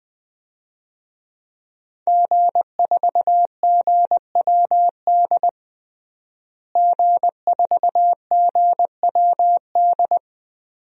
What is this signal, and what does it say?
Morse code, heard as a keyed tone, about 21 wpm, 700 Hz. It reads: G4GWD G4GWD
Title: DXCC-100 Call Signs - 5 Characters - Repeated 2x - 20wpm